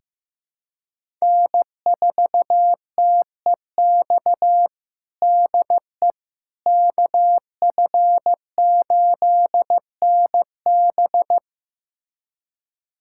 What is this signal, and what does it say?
Morse code, heard as a keyed tone, about 15 wpm, 700 Hz. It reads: N4TEX DE KF8NB
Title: USA Callsign DE Callsign - 5 Characters - 15wpm